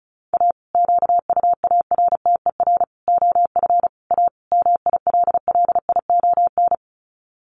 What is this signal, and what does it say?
Morse code, heard as a keyed tone, about 34 wpm, 700 Hz. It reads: A QUARTER OF A MILLION
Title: Sets of 5 Words Encore2 35wpm